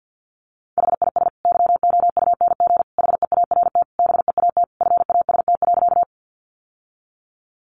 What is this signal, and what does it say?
Morse code, heard as a keyed tone, about 50 wpm, 700 Hz. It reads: HIS YOUNG HEART BEAT FAST.
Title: Sentences from Top 1300 Words 50wpm